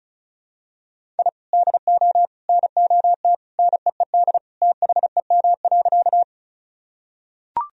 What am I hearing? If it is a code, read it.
Morse code, heard as a keyed tone, about 34 wpm, 700 Hz. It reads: I DO NOT NEED THEM.
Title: Sentences from Top 1000 Words - Review Part 1 - 35wpm